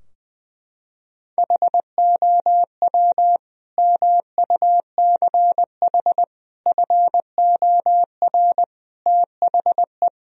Morse code, heard as a keyed tone, about 20 wpm, 700 Hz. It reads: HOW MUCH FOR THE
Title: Sentences from Top 1000 Words - Review Part 3 - 20wpm